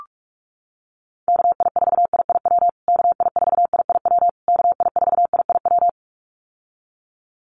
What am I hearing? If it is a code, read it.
Morse code, heard as a keyed tone, about 44 wpm, 700 Hz. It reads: KI4IIW KI4IIW KI4IIW
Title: USA Call Signs Repeated 3x 45wpm